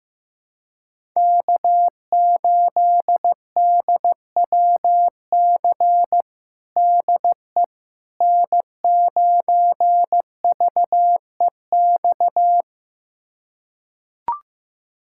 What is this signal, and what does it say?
Morse code, heard as a keyed tone, about 15 wpm, 700 Hz. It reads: K8DWC DE N9VEX
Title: USA Callsign DE Callsign - 5 Characters - 15wpm